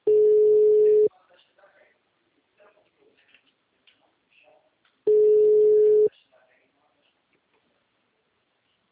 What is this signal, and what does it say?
Ringback tone (Europe, 425 Hz), with speech.